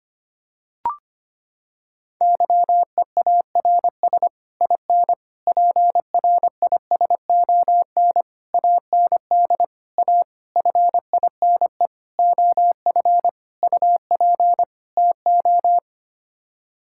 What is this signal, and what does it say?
Morse code, heard as a keyed tone, about 25 wpm, 700 Hz. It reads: YEARS IN PRISON AND A FINE OF UP TO
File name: Sets of 9 Words - 25wpm